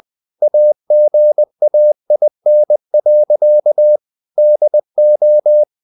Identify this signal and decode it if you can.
Morse code, heard as a keyed tone, about 20 wpm, 600 Hz. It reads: AGAIN. DO
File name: the old man and the sea part 2 20wpm